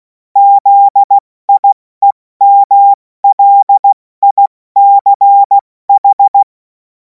Morse code, heard as a keyed tone, about 16 wpm, spaced as for 14 wpm, 800 Hz. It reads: ZIEMLICH
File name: Morsekurs / Witze / Witz 03. Februar 2026 70 F80